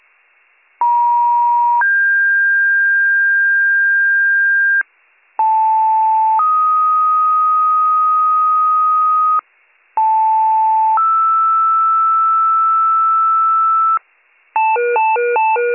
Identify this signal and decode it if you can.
Two-tone page, heard as a one-second tone, then a three-second tone, about 950 then 1600 Hz, about 880 then 1250 Hz, about 880 then 1320 Hz. It is heard from 0.8 to 14.0 s.